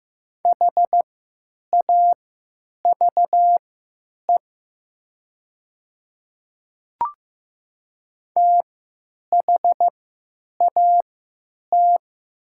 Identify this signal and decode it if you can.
Morse code, heard as a keyed tone, about 15 wpm, spaced as for 9 wpm, 700 Hz. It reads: HAVE THAT
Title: 3x Character Spacing - Top 100 Words - 15wpm